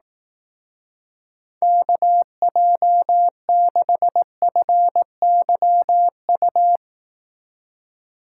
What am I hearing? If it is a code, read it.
Morse code, heard as a keyed tone, about 18 wpm, 700 Hz. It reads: KJ6FYU
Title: Speed Racing - USA 2x3 Call Signs 15wpm